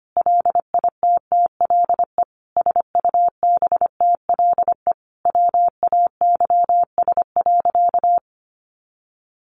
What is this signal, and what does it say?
Morse code, heard as a keyed tone, about 25 wpm, 700 Hz. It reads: LITTLE SUBTLE WAYS.
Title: A Sound Of Thunder 25wpm